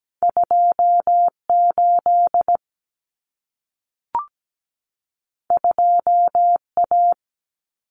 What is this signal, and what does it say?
Morse code, heard as a keyed tone, about 17 wpm, 700 Hz. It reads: 28 2A
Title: Common Cut Numbers - 2-Digit 17wpm